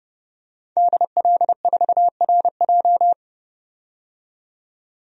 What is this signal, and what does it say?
Morse code, heard as a keyed tone, about 29 wpm, 700 Hz. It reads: DL4RJ
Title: DXCC-100 Call Signs - 5 Characters - 30wpm